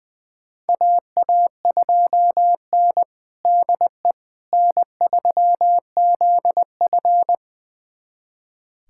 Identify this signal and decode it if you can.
Morse code, heard as a keyed tone, about 20 wpm, 700 Hz. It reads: AA2N DE N3ZF
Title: USA Callsign DE Callsign - 4 Characters - 20wpm